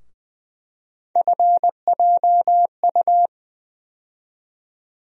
Morse code, heard as a keyed tone, about 20 wpm, 700 Hz. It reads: FJU
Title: USA Call Sign Postfixes - 3 Letters 20wpm